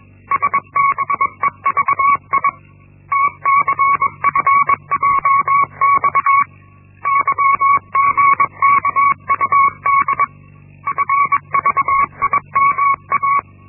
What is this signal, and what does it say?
Morse code, heard as a keyed tone, about 21 wpm, 1120 Hz. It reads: SBEVI TCFJX YGKUD FVIMA
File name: Linksmakalnis-1